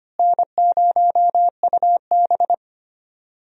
Morse code, heard as a keyed tone, about 25 wpm, 700 Hz. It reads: N0UB
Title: USA Call Signs - 4 Characters - Repeated 2x - 25wpm